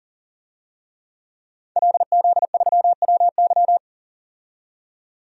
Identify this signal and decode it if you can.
Morse code, heard as a keyed tone, about 40 wpm, 700 Hz. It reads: LZ3WY
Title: DXCC-100 Call Signs - 5 Characters - 40wpm